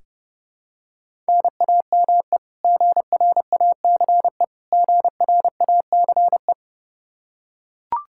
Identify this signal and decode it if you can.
Morse code, heard as a keyed tone, about 30 wpm, 700 Hz. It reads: NAME GRACE GRACE
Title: Practice QSO - Exchange 1 - Beginning - 30wpm